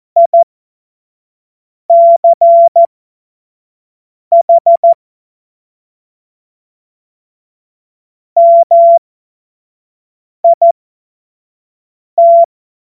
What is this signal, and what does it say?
Morse code, heard as a keyed tone, about 14 wpm, spaced as for 5 wpm, 680 Hz. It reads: ICH MIT
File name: Morsekurs / Witze / Witz 04. Mai 2021, 25 F70